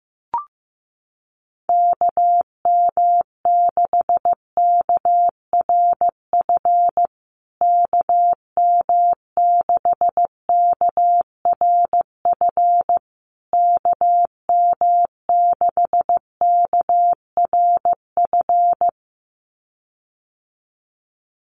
Morse code, heard as a keyed tone, about 15 wpm, 700 Hz. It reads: KM6KRF KM6KRF KM6KRF
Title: USA Call Signs Repeated 3x 15wpm